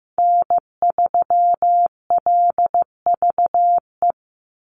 Morse code, heard as a keyed tone, about 15 wpm, 700 Hz. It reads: N3LVE
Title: USA Call Signs 15wpm